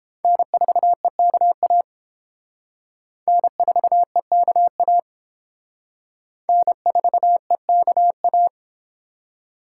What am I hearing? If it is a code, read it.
Morse code, heard as a keyed tone, about 30 wpm, 700 Hz. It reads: N4EKA N4EKA N4EKA